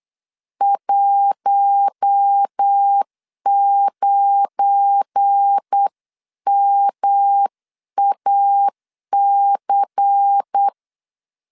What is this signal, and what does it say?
Morse code, heard as a keyed tone, about 8 wpm, 800 Hz. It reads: 19MAC
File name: LEZIONI / 9 2 8wpm 3437